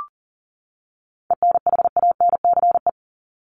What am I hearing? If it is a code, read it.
Morse code, heard as a keyed tone, about 40 wpm, 700 Hz. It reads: ENHANCE
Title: Top 1251-1500 Words 40wpm